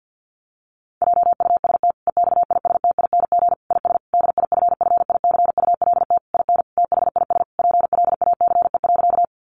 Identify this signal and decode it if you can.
Morse code, heard as a keyed tone, about 50 wpm, 700 Hz. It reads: JUST EXISTING IS DIFFICULT IN THIS PLACE.
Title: Sentences from Top 1500 Words 50wpm